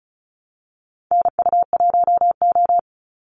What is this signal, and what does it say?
Morse code, heard as a keyed tone, about 34 wpm, 700 Hz. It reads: NU1O
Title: Call Signs - 35wpm